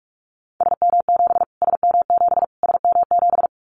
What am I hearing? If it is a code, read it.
Morse code, heard as a keyed tone, about 46 wpm, 700 Hz. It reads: SM7 SM7 SM7
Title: Callsign Prefixes for Top 25 DXCC - Repeated 3x - 45wpm